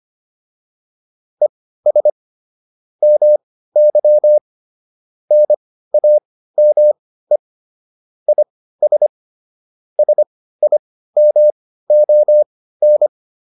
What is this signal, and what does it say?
Morse code, heard as a keyed tone, about 25 wpm, spaced as for 15 wpm, 600 Hz. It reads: ES MY NAME IS SIMON